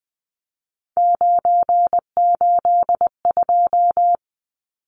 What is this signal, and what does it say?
Morse code, heard as a keyed tone, about 20 wpm, 700 Hz. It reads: 982
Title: Numbers - 3-Digit 20wpm